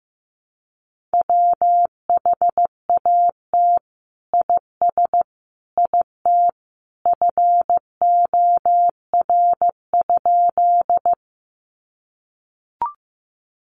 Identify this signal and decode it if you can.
Morse code, heard as a keyed tone, about 15 wpm, 700 Hz. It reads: WHAT IS IT FOR?
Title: Sentences from Top 100 Words 15wpm